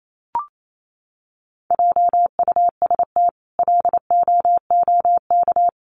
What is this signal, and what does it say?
Morse code, heard as a keyed tone, about 28 wpm, 700 Hz. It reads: JUST LOOK
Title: Sentences from Top 200 Words 28wpm